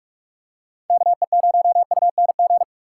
Morse code, heard as a keyed tone, about 43 wpm, 700 Hz. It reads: KE0UNG
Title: USA Call Signs - 6 Characters - 2x Character Spacing - 45wpm